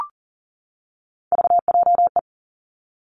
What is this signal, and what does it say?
Morse code, heard as a keyed tone, about 39 wpm, 700 Hz. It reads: VJE
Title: Three Challenging Letter-Number Combinations 40wpm